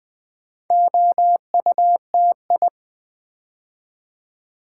Morse code, heard as a keyed tone, about 20 wpm, 700 Hz. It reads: OUTI
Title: Build A Word - 2 Letters At A Time - 6-Letter Words - 20wpm